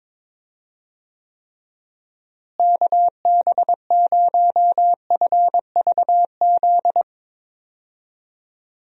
Morse code, heard as a keyed tone, about 22 wpm, 700 Hz. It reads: KB0FVZ
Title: USA Call Signs - 6 Characters - Repeated 3x - 22wpm